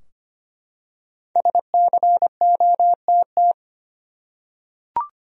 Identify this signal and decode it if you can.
Morse code, heard as a keyed tone, about 25 wpm, 700 Hz. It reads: SCOTT
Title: USA Common Names - Max 5 Letters - 25wpm